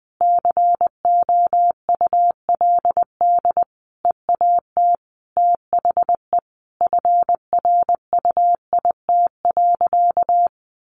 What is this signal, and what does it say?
Morse code, heard as a keyed tone, about 20 wpm, 700 Hz. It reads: COULD EAT THE FRUIT.
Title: Sentences from Top 1300 Words 20wpm